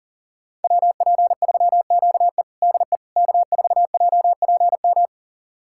Morse code, heard as a keyed tone, about 37 wpm, 700 Hz. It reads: WP3QE DE K4JPM
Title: USA Callsign DE Callsign - 5 Characters - 40wpm